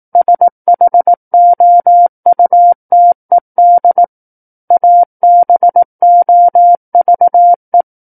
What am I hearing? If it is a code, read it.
Morse code, heard as a keyed tone, about 18 wpm, 700 Hz. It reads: SHOUTED ABOVE
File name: 18 WPM